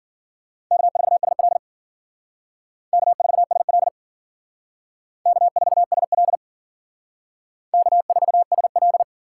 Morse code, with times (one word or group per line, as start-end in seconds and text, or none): 0.7-1.6 K4SL
2.9-3.9 K4SL
5.3-6.4 K4SL
7.7-9.0 K4SL